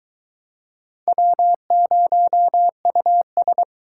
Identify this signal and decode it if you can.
Morse code, heard as a keyed tone, about 22 wpm, 700 Hz. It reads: W0US